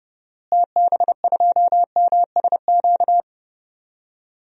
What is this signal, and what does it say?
Morse code, heard as a keyed tone, about 30 wpm, 700 Hz. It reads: TB2MSQ
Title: DXCC-100 Call Signs - 6 Characters - Repeated 3x - 30wpm